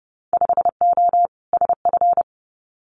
Morse code, heard as a keyed tone, about 30 wpm, 700 Hz. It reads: 5O SF
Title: Winter Field Day Exchanges Encore 30wpm